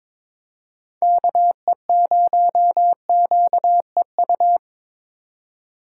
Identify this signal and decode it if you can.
Morse code, heard as a keyed tone, about 21 wpm, 700 Hz. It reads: KE0QEU
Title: USA Call Signs - 6 Characters - 2x Character Spacing - 22wpm